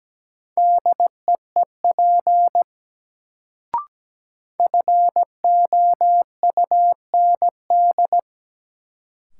Morse code, heard as a keyed tone, about 17 wpm, 700 Hz. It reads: DEEP FOUND
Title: Sets of 5 Words Encore - New Word Drill - 17wpm